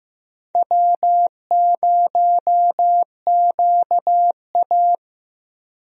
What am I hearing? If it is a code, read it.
Morse code, heard as a keyed tone, about 15 wpm, 700 Hz. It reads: W0QA
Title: USA Call Signs - 4 Characters - Repeated 3x - 15wpm